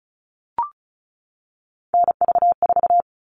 Morse code, heard as a keyed tone, about 34 wpm, 700 Hz. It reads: NV4
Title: All Cut Numbers - 3-Digit 35wpm